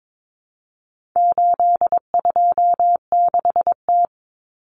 Morse code, heard as a keyed tone, about 22 wpm, 700 Hz. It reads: 826T
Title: Common Cut Numbers - 4-Digit 22wpm